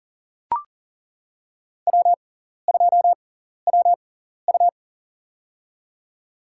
Morse code, heard as a keyed tone, about 37 wpm, spaced as for 14 wpm, 700 Hz. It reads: W2WU
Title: USA Call Signs - 4 Characters - 6x Character Spacing - 40wpm